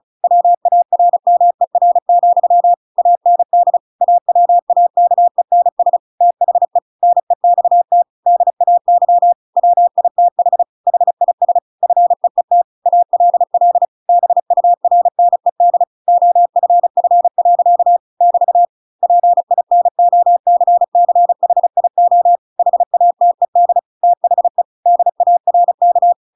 Morse code, heard as a keyed tone, about 34 wpm, 700 Hz. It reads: WARMER, AND AWAKENS THE NEXT DAY WITH HIS FEET ALL BURNED OFF. <BT> PINOCCHIO HATED THE DARK
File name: Pinocchio / Pinocchio ch6 35 35 ik1hge